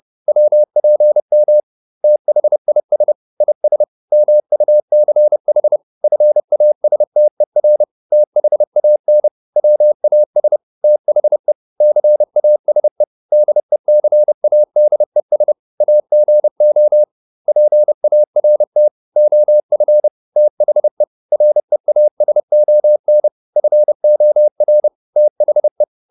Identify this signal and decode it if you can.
Morse code, heard as a keyed tone, about 30 wpm, 600 Hz. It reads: WPM THIS IS MUCH FASTER THAN WAS THE CASE DECADES AGO PART OF THE REASON FOR THE